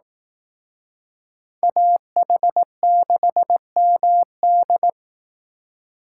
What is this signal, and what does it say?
Morse code, heard as a keyed tone, about 18 wpm, 700 Hz. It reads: AH6MD
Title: Speed Racing - USA 2x2 Call Signs 15wpm